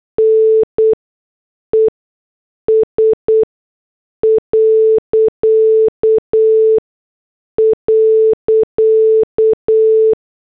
Morse code, heard as a keyed tone, about 8 wpm, spaced as for 6 wpm, 440 Hz. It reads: NES..